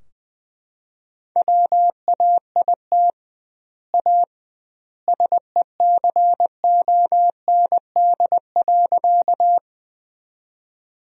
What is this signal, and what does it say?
Morse code, heard as a keyed tone, about 20 wpm, 700 Hz. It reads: WAIT A SECOND.
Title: 2x Word-Spacing - Sentences from Top 500 Words - 20wpm